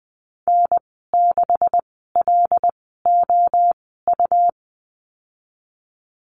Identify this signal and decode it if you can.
Morse code, heard as a keyed tone, about 20 wpm, spaced as for 14 wpm, 700 Hz. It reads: N6LOU
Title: 161 Lesson 26g - Letter g - Call Signs 20wpm